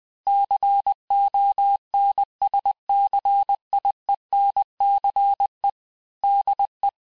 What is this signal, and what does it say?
Morse code, heard as a keyed tone, about 20 wpm, 800 Hz. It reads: CONSCIENCE DE
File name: cours-du-mercredi / audios / cours-858-texte-clais-vitesse-19mpm